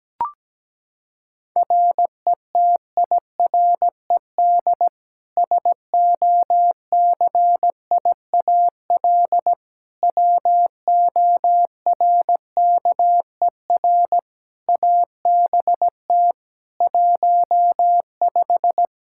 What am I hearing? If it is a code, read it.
Morse code, heard as a keyed tone, about 17 wpm, 700 Hz. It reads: RETIRED SOCIAL WORKER ABT 15
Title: K1USN SST QSO Practice 17wpm